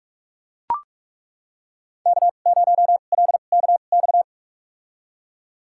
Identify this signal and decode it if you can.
Morse code, heard as a keyed tone, about 42 wpm, spaced as for 31 wpm, 700 Hz. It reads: K0LKX
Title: USA Call Signs - 5 Characters - 2x Character Spacing - 45wpm